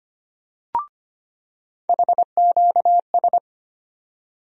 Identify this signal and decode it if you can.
Morse code, heard as a keyed tone, about 25 wpm, 700 Hz. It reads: HQS
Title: USA Call Sign Postfixes - 3 Letters 25wpm